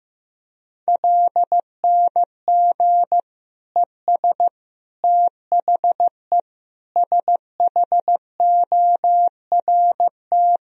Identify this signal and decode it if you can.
Morse code, heard as a keyed tone, about 15 wpm, 700 Hz. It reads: LNG ES THE SHORT
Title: Sentences from Top 400 Words and Abbreviations 15wpm